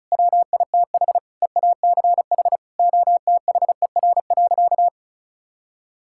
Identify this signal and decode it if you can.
Morse code, heard as a keyed tone, about 36 wpm, 700 Hz. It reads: WITH EACH OTHER.
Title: Sentences from Top 1000 Words - Review Part 6 - 35wpm